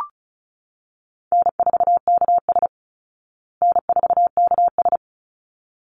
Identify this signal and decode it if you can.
Morse code, heard as a keyed tone, about 34 wpm, 700 Hz. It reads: N4KS N4KS
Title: Easy Call Signs Repeated 2x - 35wpm